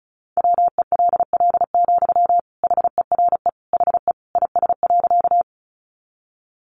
Morse code, heard as a keyed tone, about 34 wpm, 700 Hz. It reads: WELL, HERE HE IS.
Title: Sentences from Top 200 Words 35wpm